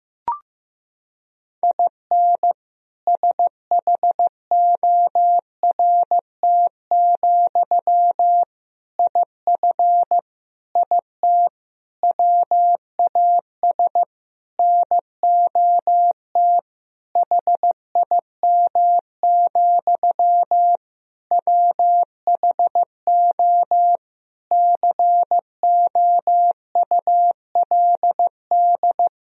Morse code, heard as a keyed tone, about 16 wpm, 700 Hz. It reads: IN SHORT, IF IT WAS NOT HIM, WHO COULD
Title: Sentences from Top 1000 Words - Review Part 2 - 15wpm